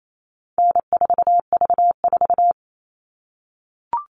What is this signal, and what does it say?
Morse code, heard as a keyed tone, about 28 wpm, 700 Hz. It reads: N4V4
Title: All Cut Numbers - 4-Digit 28wpm